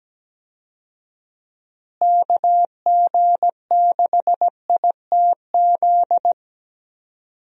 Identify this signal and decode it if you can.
Morse code, heard as a keyed tone, about 17 wpm, 700 Hz. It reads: KG6ITZ
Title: USA Call Signs - 6 Characters - Repeated 2x - 17wpm